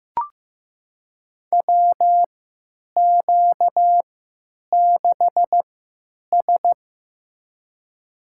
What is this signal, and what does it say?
Morse code, heard as a keyed tone, about 15 wpm, spaced as for 9 wpm, 700 Hz. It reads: WQ6S
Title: USA Call Signs - 3x Character Spacing - 15wpm